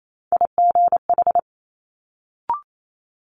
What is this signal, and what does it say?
Morse code, heard as a keyed tone, about 28 wpm, 700 Hz. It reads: IGH
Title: Multi-letter Phonemes Quick Fire 28wpm